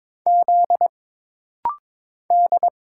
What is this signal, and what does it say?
Morse code, heard as a keyed tone, about 22 wpm, 700 Hz. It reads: Z D
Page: This rapid-fire timing is geared to push your ICR (Instant Character Recognition) skills.